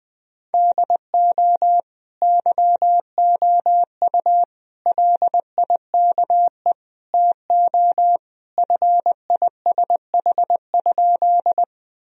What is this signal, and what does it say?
Morse code, heard as a keyed tone, about 21 wpm, 700 Hz. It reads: DO YOU LIKE TO FISH?
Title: Sentences from Top 1000 Words - Review Part 1 - 20wpm